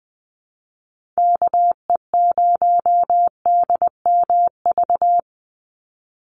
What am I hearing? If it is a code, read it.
Morse code, heard as a keyed tone, about 20 wpm, 700 Hz. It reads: KE0DMV
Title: USA Call Signs 20wpm